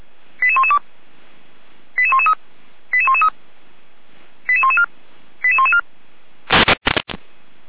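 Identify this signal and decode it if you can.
Five-tone selective call (ZVEI), heard as quick runs of five tones, tones of about 70 ms. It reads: 88112 88113 88113 88114 88114
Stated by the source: ZVEI five-tone selective call received with an SDR.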